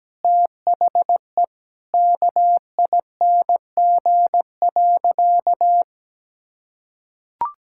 Morse code, heard as a keyed tone, about 17 wpm, 700 Hz. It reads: THE KING.
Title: Sentences from Top 1000 Words - Review Part 5 - 17wpm